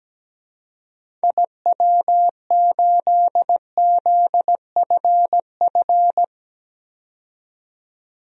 Morse code, heard as a keyed tone, about 17 wpm, 700 Hz. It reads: IW8ZFF
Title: DXCC-100 Call Signs - 6 Characters - 17wpm